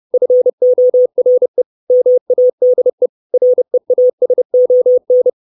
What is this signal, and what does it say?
Morse code, heard as a keyed tone, about 30 wpm, 500 Hz. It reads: FORE MADE REASON
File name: 632 Words 30WPM